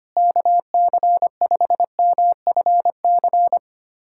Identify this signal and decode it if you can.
Morse code, heard as a keyed tone, about 25 wpm, 700 Hz. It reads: KC5MFC
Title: USA Callsign DE Callsign - 6 Characters - 25wpm